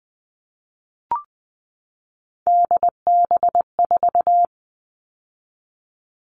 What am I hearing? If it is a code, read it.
Morse code, heard as a keyed tone, about 20 wpm, 700 Hz. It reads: DB4
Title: All Cut Numbers - 3-Digit 20wpm